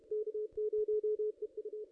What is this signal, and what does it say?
Morse code, heard as a keyed tone, about 33 wpm, 430 Hz. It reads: K0EU
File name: ss-cw-k5tr-op-n5rz